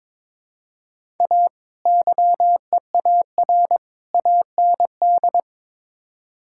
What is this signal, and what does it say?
Morse code, heard as a keyed tone, about 22 wpm, 700 Hz. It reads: A YEAR AND
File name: practice / 22wpm / Build A Phrase - 5-Word Phrases - 22wpm